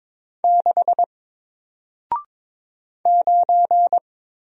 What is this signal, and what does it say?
Morse code, heard as a keyed tone, about 23 wpm, 700 Hz. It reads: 6 9
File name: Single-Letter-Number-22wpm